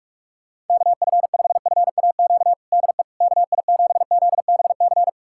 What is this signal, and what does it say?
Morse code, heard as a keyed tone, about 47 wpm, 700 Hz. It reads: KF5FAQ DE KI7ZBC
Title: USA Callsign DE Callsign 45wpm